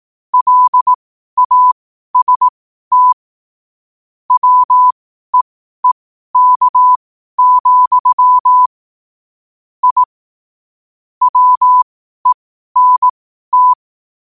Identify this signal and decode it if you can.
Morse code, heard as a keyed tone, about 18 wpm, spaced as for 12 wpm, 1000 Hz. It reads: LAST WEEK, I WENT
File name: qotdfiles / 12wpm